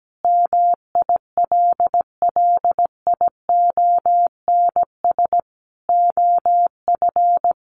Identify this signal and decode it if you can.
Morse code, heard as a keyed tone, about 17 wpm, 700 Hz. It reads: MILLIONS OF
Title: Sets of 5 Words Encore 17wpm